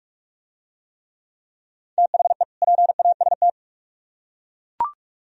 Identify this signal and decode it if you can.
Morse code, heard as a keyed tone, about 41 wpm, 700 Hz. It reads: THE PAST